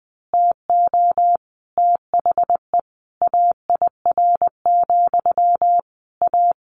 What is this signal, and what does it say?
Morse code, heard as a keyed tone, about 20 wpm, 700 Hz. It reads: TO THE AIR, A
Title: A Sound Of Thunder 20wpm